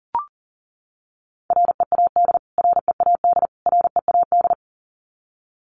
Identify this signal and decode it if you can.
Morse code, heard as a keyed tone, about 40 wpm, 700 Hz. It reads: READ READ READ
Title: Top 300 Words Repeated 40wpm